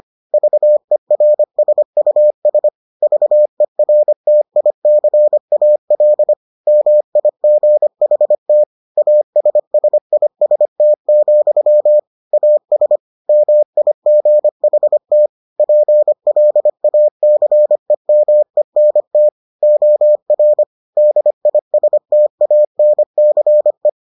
Morse code, heard as a keyed tone, about 25 wpm, 600 Hz. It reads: VERSUS VERTICAL MIGHT ASSIST, AS MIGHT PLACEMENT OR DISTANCE